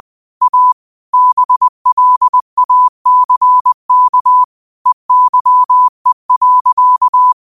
Morse code, heard as a keyed tone, about 20 wpm, 1000 Hz. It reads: A BLACK EYE.